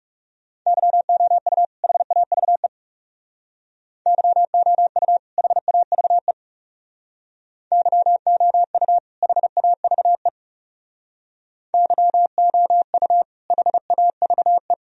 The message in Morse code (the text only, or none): YOU HAVE YOU HAVE YOU HAVE YOU HAVE